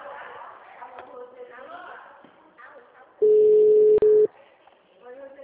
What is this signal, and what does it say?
Ringback tone (Europe, 425 Hz), with speech.